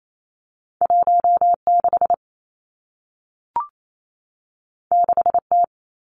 Morse code, heard as a keyed tone, about 28 wpm, 700 Hz. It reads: 1 6 6 T
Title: Two Challenging Letter-Number Combinations 28wpm